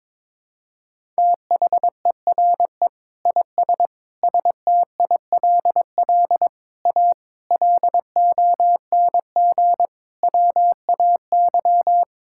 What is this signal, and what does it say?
Morse code, heard as a keyed tone, about 22 wpm, 700 Hz. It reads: THERE IS STILL A LONG WAY
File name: practice / 22wpm / Sets of 8 Words - 22wpm